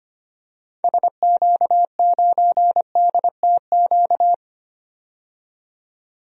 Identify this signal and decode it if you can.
Morse code, heard as a keyed tone, about 25 wpm, 700 Hz. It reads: SQ9DTQ
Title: DXCC-100 Call Signs - 6 Characters - 25wpm